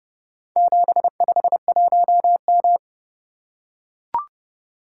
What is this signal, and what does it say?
Morse code, heard as a keyed tone, about 30 wpm, 700 Hz. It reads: 751M